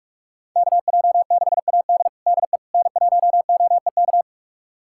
Morse code, heard as a keyed tone, about 44 wpm, 700 Hz. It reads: KJ6AD DE N1OEK
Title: USA Callsign DE Callsign - 5 Characters - 45wpm